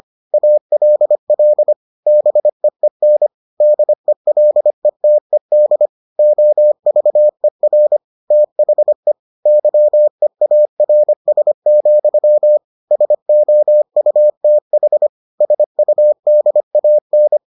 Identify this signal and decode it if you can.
Morse code, heard as a keyed tone, about 25 wpm, 600 Hz. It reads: ALL BEEN DELETED OVER THE YEARS, SOUTH SUDAN